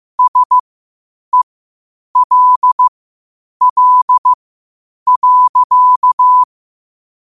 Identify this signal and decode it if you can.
Morse code, heard as a keyed tone, about 15 wpm, spaced as for 8 wpm, 1000 Hz. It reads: SELL.
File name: qotdfiles / 8wpm